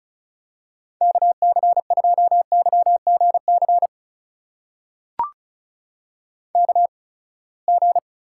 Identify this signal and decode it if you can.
Morse code, heard as a keyed tone, about 36 wpm, spaced as for 24 wpm, 700 Hz. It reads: KC2YGC KG
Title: USA Call Signs - 6 Characters - 8x Character Spacing - 35wpm